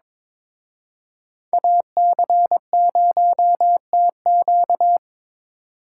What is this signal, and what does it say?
Morse code, heard as a keyed tone, about 26 wpm, 700 Hz. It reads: AC0TQ
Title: Speed Racing - USA 2x2 Call Signs 22wpm